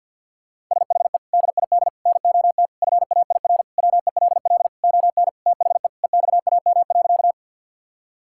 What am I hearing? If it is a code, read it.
Morse code, heard as a keyed tone, about 46 wpm, 690 Hz. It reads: SHE DID NOT FAIR WELL ON THE EXAM.
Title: Sentences from Top 1000 Words - Review Part 5 - 50wpm